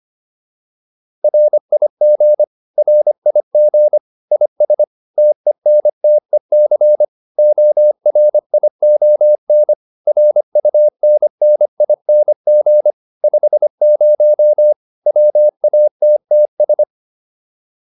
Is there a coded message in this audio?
Yes: Morse code from 1.2 to 16.8 s, heard as a keyed tone, about 25 wpm, 600 Hz. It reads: RIG RIG IS TENTEC ORION RUNNING 50 WATTS